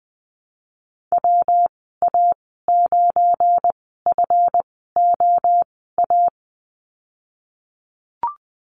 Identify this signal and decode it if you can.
Morse code, heard as a keyed tone, about 20 wpm, spaced as for 15 wpm, 700 Hz. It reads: WA9FOA
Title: 193 Lesson 31g - Letter v - Call Signs 20wpm